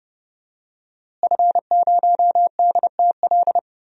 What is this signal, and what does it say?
Morse code, heard as a keyed tone, about 30 wpm, 700 Hz. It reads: F0DTL
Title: DXCC-100 Call Signs - 5 Characters - Repeated 3x - 30wpm